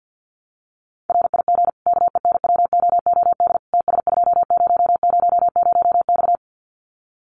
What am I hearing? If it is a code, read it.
Morse code, heard as a keyed tone, about 50 wpm, 700 Hz. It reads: RIG KENWOOD TS2000X
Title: Common QSO Phrases 50wpm